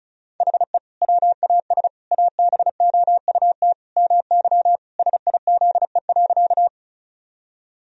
Morse code, heard as a keyed tone, about 36 wpm, 700 Hz. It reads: HE WAS ABOUT MY SIZE.
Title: Sentences from Top 1000 Words - Review Part 2 - 35wpm